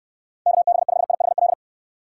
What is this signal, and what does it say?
Morse code, heard as a keyed tone, about 69 wpm, 700 Hz. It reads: KB5IHL